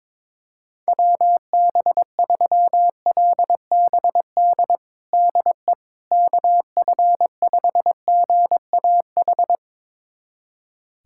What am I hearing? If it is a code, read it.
Morse code, heard as a keyed tone, about 22 wpm, 700 Hz. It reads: WB3LBD DE KF5GAH
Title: USA Callsign DE Callsign 22wpm